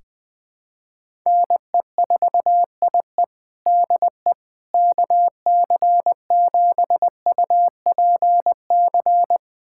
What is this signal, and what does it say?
Morse code, heard as a keyed tone, about 21 wpm, 700 Hz. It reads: NE4IE DE KC7UPC
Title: USA Callsign DE Callsign 20wpm